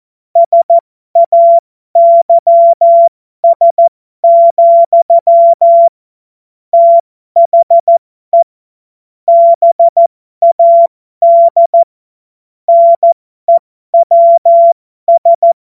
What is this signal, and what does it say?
Morse code, heard as a keyed tone, about 14 wpm, spaced as for 12 wpm, 680 Hz. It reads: SAYS, THE BAD NEWS
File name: Morsekurs / Witze